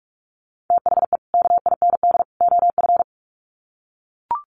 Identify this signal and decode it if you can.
Morse code, heard as a keyed tone, about 44 wpm, 700 Hz. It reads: THE KIND OF
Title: Sets of 3 Words 45wpm